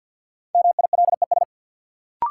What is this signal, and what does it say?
Morse code, heard as a keyed tone, about 53 wpm, 700 Hz. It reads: MILES
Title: Sentences from Top 1000 Words - Review Part 3 - New Word Drill - 50wpm